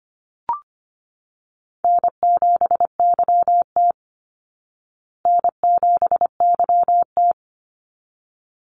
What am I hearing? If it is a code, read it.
Morse code, heard as a keyed tone, about 25 wpm, 700 Hz. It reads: N7YT N7YT
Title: Call Signs Repeated 3x - 25wpm